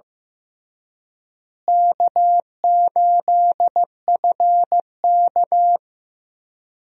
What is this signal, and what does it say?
Morse code, heard as a keyed tone, about 15 wpm, 700 Hz. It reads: K8FK
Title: USA Call Signs - 4 Characters - Repeated 2x - 15wpm